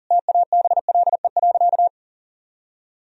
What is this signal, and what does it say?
Morse code, heard as a keyed tone, about 40 wpm, 700 Hz. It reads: TABLE.
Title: Sentences from Top 1000 Words - Review Part 3 - 40wpm